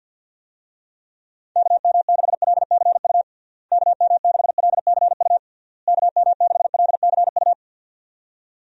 Morse code, heard as a keyed tone, about 51 wpm, 700 Hz. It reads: KM6LCU KM6LCU KM6LCU
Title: USA Call Signs - 6 Characters - Repeated 3x - 50wpm